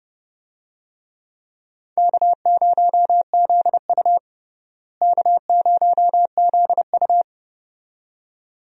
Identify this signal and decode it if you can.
Morse code, heard as a keyed tone, about 32 wpm, 700 Hz. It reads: K0ZU K0ZU
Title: USA Call Signs - 4 Characters - Repeated 2x - 30wpm